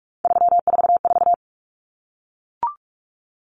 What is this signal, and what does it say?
Morse code, heard as a keyed tone, about 44 wpm, 700 Hz. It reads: 344
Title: Numbers - 3-Digit 45wpm